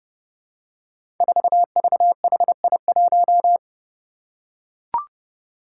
Morse code, heard as a keyed tone, about 30 wpm, 700 Hz. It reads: 4VHI1
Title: Five Challenging Letter-Number Combinations 30wpm